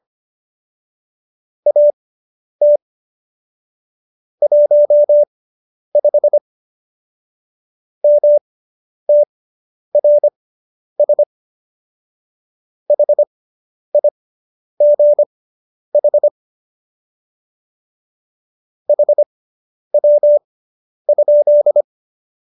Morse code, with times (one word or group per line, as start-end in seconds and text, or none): 1.7-2.8 AT
4.4-6.4 15
8.0-11.2 MTRS
12.9-16.3 HIGH
18.9-21.8 HW?